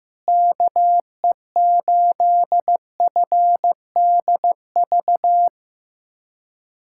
Morse code, heard as a keyed tone, about 15 wpm, 700 Hz. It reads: KE8FDV
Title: USA Callsign DE Callsign - 6 Characters - 15wpm